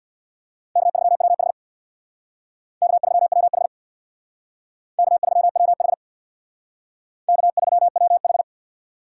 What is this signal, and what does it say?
Morse code, heard as a keyed tone, about 62 wpm, 690 Hz. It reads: K3WH K3WH K3WH K3WH
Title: Speed Racing - USA 1x2 Call Signs 50wpm